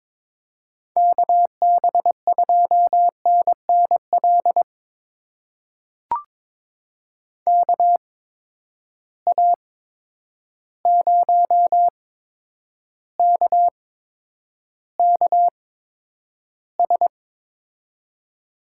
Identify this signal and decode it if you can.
Morse code, heard as a keyed tone, about 22 wpm, spaced as for 9 wpm, 700 Hz. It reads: KB2NNL KA0KKS
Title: USA Call Signs - 6 Characters - 8x Character Spacing - 22wpm